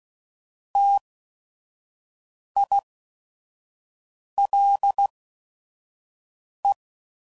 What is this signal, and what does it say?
Morse code, heard as a keyed tone, about 16 wpm, spaced as for 5 wpm, 800 Hz. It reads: TILE